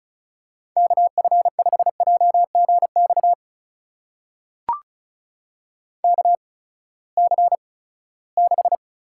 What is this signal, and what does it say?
Morse code, heard as a keyed tone, about 36 wpm, spaced as for 20 wpm, 700 Hz. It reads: KF5JGX KC6
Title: USA Call Signs - 6 Characters - 8x Character Spacing - 35wpm